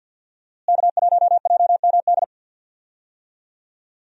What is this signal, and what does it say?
Morse code, heard as a keyed tone, about 50 wpm, 700 Hz. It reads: K1JMD
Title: USA Call Signs - 5 Characters - Repeated 2x - 50wpm